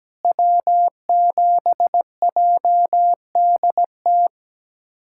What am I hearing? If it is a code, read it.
Morse code, heard as a keyed tone, about 17 wpm, 700 Hz. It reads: W7JDT
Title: USA Callsign DE Callsign - 5 Characters - 17wpm